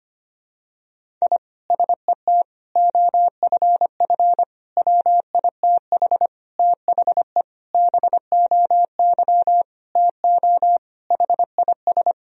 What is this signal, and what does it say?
Morse code, heard as a keyed tone, about 25 wpm, 700 Hz. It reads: I SET OFF WITH THE BOY TO HIS
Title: Sentences from Top 1000 Words - Review Part 1 - 25wpm